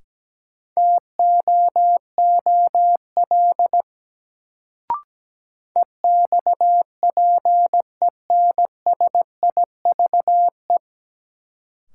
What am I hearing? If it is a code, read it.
Morse code, heard as a keyed tone, about 17 wpm, 700 Hz. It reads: TOOL EXPENSIVE
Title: Sentences from Top 1000 Words - Review Part 6 - New Word Drill - 17wpm